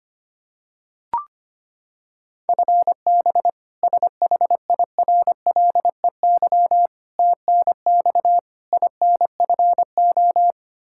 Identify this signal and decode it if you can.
Morse code, heard as a keyed tone, about 25 wpm, 700 Hz. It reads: FB SHIRLEY TNX INFO
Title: K1USN SST QSO Practice 25wpm